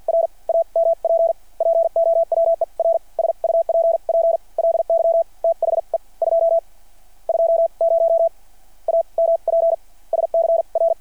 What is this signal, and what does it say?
Morse code, heard as a keyed tone, about 53 wpm, spaced as for 34 wpm, 660 Hz. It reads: R AMP POWER SUPPLY THE 220 AMPS CR
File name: archived / CRSnet